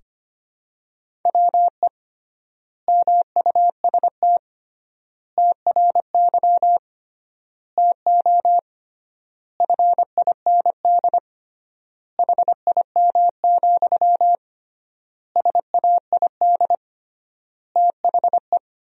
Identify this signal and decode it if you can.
Morse code, heard as a keyed tone, about 25 wpm, 700 Hz. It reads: WE MUST TRY TO FIND HIM, SAID THE
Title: 3x Word-Spacing - Sentences from Top 400 Words - 25wpm